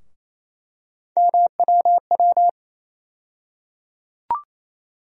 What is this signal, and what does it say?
Morse code, heard as a keyed tone, about 27 wpm, 700 Hz. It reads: MWW
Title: USA Call Sign Postfixes - 3 Letters 28wpm